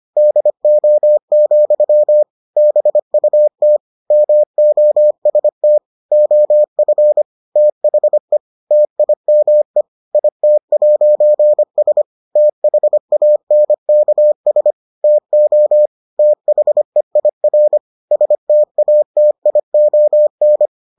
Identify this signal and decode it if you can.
Morse code, heard as a keyed tone, about 25 wpm, 600 Hz. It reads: DO, BUT MOST OF THE TIME IT'S THANKS TO THEIR STATION